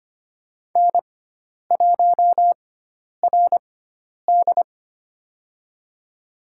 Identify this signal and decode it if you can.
Morse code, heard as a keyed tone, about 25 wpm, spaced as for 10 wpm, 700 Hz. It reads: N1RD
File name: Lesson 16g - Letter c - Call Signs 25wpm